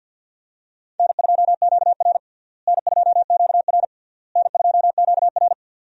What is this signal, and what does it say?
Morse code, heard as a keyed tone, about 51 wpm, 700 Hz. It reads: N2QR N2QR N2QR
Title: USA Call Signs - 4 Characters - Repeated 3x - 50wpm